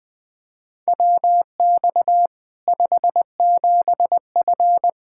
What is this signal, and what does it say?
Morse code, heard as a keyed tone, about 21 wpm, 700 Hz. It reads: WX 57F
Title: K1USN SST QSO Practice 20wpm